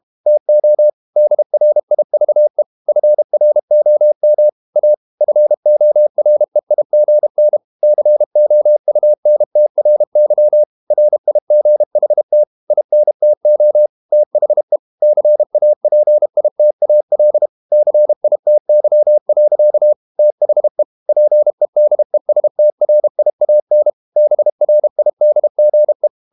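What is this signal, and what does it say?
Morse code, heard as a keyed tone, about 32 wpm, 600 Hz. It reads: TO DRIVE FROM A FOREIGN COUNTRY RIGHT INTO THE CAPITAL CITY. THE PEDESTRIAN BRIDGE
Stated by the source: Morse code practice at 32 wpm from CW med Gnister, reading Ken Follett's novel Never in English.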